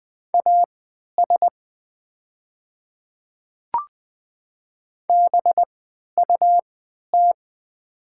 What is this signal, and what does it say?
Morse code, heard as a keyed tone, about 20 wpm, spaced as for 11 wpm, 690 Hz. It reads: AS BUT
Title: 3x Character Spacing - Top 100 Words - 20wpm